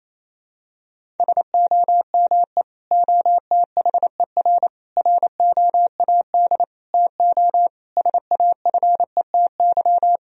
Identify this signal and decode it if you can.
Morse code, heard as a keyed tone, about 28 wpm, 700 Hz. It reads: SOME OTHER ROAD TO SAFETY
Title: Sentences from Top 1000 Words - Review Part 4 - 28wpm